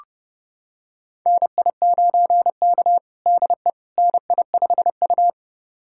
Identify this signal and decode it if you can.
Morse code, heard as a keyed tone, about 32 wpm, 700 Hz. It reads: NI9K DE NI5U
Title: USA Callsign DE Callsign - 4 Characters - 30wpm